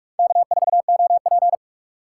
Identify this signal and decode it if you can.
Morse code, heard as a keyed tone, about 42 wpm, 700 Hz. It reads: K4OP